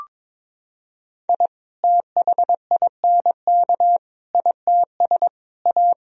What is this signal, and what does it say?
Morse code, heard as a keyed tone, about 22 wpm, 700 Hz. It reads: I THINK ITS A
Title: Sentences from Top 1000 Words - Review Part 1 - 22wpm